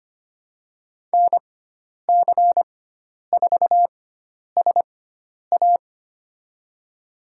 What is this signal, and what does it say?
Morse code, heard as a keyed tone, about 25 wpm, spaced as for 10 wpm, 700 Hz. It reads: NC4SA
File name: Lesson 16g - Letter c - Call Signs 25wpm